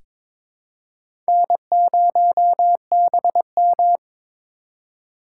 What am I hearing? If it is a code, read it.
Morse code, heard as a keyed tone, about 22 wpm, 700 Hz. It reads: N0BM
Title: USA Call Signs - 4 Characters - 2x Character Spacing - 22wpm